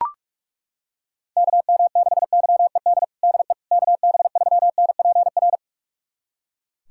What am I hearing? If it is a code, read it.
Morse code, heard as a keyed tone, about 41 wpm, 690 Hz. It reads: KM6YED DE KB3NPR
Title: USA Callsign DE Callsign - 6 Characters - 45wpm